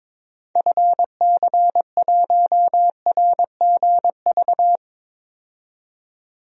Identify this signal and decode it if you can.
Morse code, heard as a keyed tone, about 22 wpm, 700 Hz. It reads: FC1RGV
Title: DXCC-100 Call Signs - 6 Characters - 22wpm